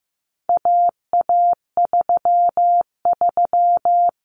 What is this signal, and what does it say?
Morse code, heard as a keyed tone, about 15 wpm, 700 Hz. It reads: AA33
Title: Common Cut Numbers - 4-Digit 15wpm